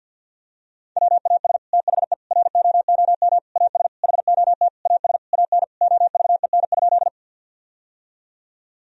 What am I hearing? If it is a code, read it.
Morse code, heard as a keyed tone, about 48 wpm, 700 Hz. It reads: WAS THE ROOM AS HOT AS AN OVEN?
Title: Sentences from Top 1000 Words - Review Part 4 - 50wpm